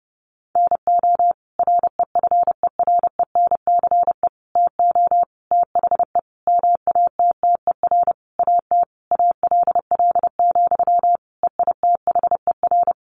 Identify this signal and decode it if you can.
Morse code, heard as a keyed tone, about 30 wpm, 700 Hz. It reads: NO REFERENCE TO THE MATTER AT ALL, EITHER
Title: Sentences from Top 1200 Words 30wpm